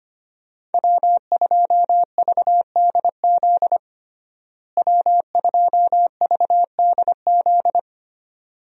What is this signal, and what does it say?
Morse code, heard as a keyed tone, about 25 wpm, 700 Hz. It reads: W2VDZ W2VDZ
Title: USA Call Signs - 5 Characters - Repeated 2x - 25wpm